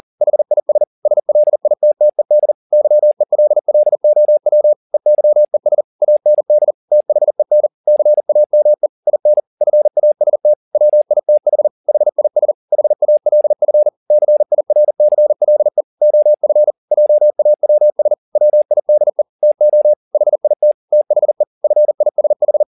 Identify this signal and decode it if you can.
Morse code, heard as a keyed tone, about 40 wpm, 600 Hz. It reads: HIS SLITTED YELLOW EYES AND THEN CAME IN FAST WITH HIS HALF CIRCLE OF JAWS WIDE TO HIT THE FISH
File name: the old man and the sea 2 40wpm